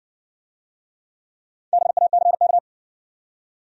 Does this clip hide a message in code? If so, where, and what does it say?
Yes: Morse code from 1.7 to 2.6 s, heard as a keyed tone, about 60 wpm, 700 Hz. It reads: BACK